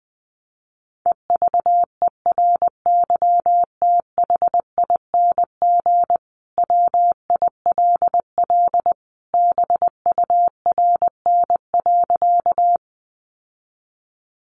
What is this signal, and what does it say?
Morse code, heard as a keyed tone, about 20 wpm, 700 Hz. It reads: EVERYTHING WILL BURN.
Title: Sentences from Top 1300 Words 20wpm